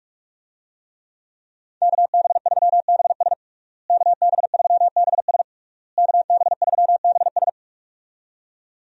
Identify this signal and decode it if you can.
Morse code, heard as a keyed tone, about 46 wpm, 700 Hz. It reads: KB3BS KB3BS KB3BS
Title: USA Call Signs - 5 Characters - Repeated 3x - 45wpm